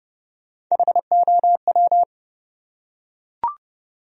Morse code, heard as a keyed tone, about 31 wpm, 700 Hz. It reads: HOW
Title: Top 100 Words Repeated 30wpm